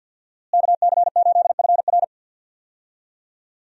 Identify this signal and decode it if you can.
Morse code, heard as a keyed tone, about 52 wpm, 700 Hz. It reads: KC8FR
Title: USA Call Signs - 5 Characters - Repeated 2x - 50wpm